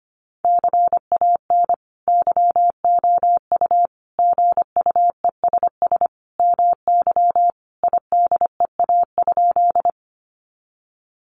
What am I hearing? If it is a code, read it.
Morse code, heard as a keyed tone, about 25 wpm, 700 Hz. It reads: CAN YOU GUESS MY IDEA?
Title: Sentences from Top 1500 Words 25wpm